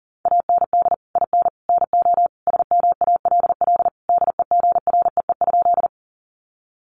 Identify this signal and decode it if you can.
Morse code, heard as a keyed tone, about 40 wpm, 700 Hz. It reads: AND IN NO SMALL DEGREE?
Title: Sentences from Top 1500 Words 40wpm